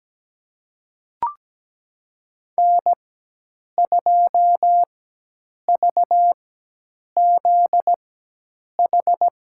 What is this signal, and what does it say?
Morse code, heard as a keyed tone, about 17 wpm, spaced as for 8 wpm, 700 Hz. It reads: N2VZH
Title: USA Call Signs - 5 Characters - 4x Character Spacing - 17wpm